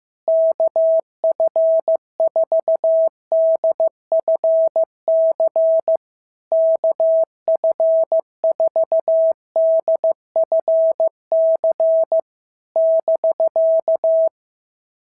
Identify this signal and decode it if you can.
Morse code, heard as a keyed tone, about 14 wpm, 650 Hz. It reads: KF4DFC KF4DFC <BK>
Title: SOTA QSO Practice 15wpm